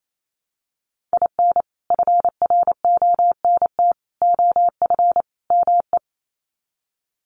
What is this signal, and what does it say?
Morse code, heard as a keyed tone, about 28 wpm, 700 Hz. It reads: IN FRONT OF ME
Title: Sets of 4 Words 28wpm